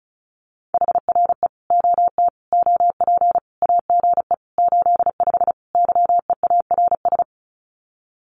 Morse code, heard as a keyed tone, about 34 wpm, 700 Hz. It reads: HRE OT OP AGE 85 YEARS
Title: Common QSO Phrases 35wpm